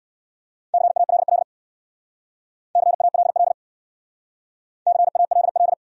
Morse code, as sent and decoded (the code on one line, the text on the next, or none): -.-. .- .-.. .-.. / -.-. .- .-.. .-.. / -.-. .- .-.. .-..
CALL CALL CALL